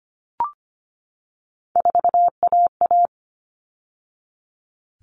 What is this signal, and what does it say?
Morse code, heard as a keyed tone, about 25 wpm, 700 Hz. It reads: 4AA
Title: All Cut Numbers - 3-Digit 25wpm